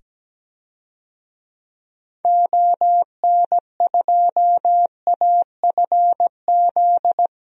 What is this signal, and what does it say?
Morse code, heard as a keyed tone, about 17 wpm, 700 Hz. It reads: ON2AFZ
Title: DXCC-100 Call Signs - 6 Characters - 17wpm